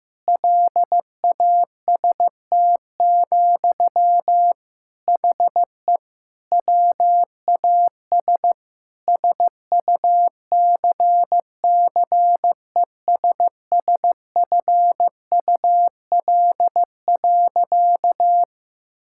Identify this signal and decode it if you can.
Morse code, heard as a keyed tone, about 15 wpm, 700 Hz. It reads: LAST, HE WAS SUCCESSFUL.
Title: Sentences from Top 1000 Words - Review Part 6 - 15wpm